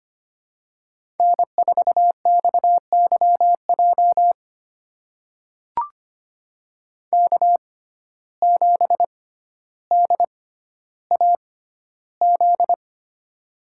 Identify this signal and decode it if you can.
Morse code, heard as a keyed tone, about 25 wpm, spaced as for 13 wpm, 700 Hz. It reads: N4XYJ K7DAZ
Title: USA Call Signs - 5 Characters - 6x Character Spacing - 25wpm